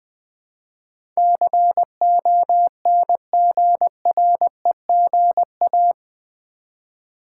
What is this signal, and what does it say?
Morse code, heard as a keyed tone, about 20 wpm, 700 Hz. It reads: CONGREGA
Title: Build A Word - 4 Letters At A Time - 12-Letter Words - 20wpm